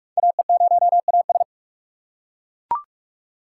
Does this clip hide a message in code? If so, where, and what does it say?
Yes: Morse code from 0.2 to 1.4 s, heard as a keyed tone, about 41 wpm, 700 Hz. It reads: AE0AS